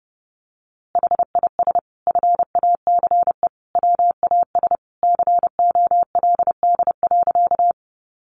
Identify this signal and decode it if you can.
Morse code, heard as a keyed tone, about 30 wpm, 700 Hz. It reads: HIS FACE WAS COLD.
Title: A Sound Of Thunder 30wpm